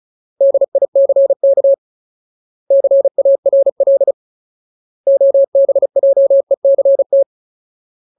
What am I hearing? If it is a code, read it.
Morse code, heard as a keyed tone, about 34 wpm, 550 Hz. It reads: DICK CARL OBJECT
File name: Ultimate Word List / 35WPM Ultimate